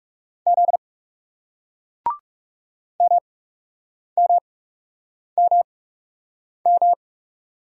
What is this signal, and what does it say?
Morse code, heard as a keyed tone, about 38 wpm, spaced as for 8 wpm, 700 Hz. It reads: Z MMMM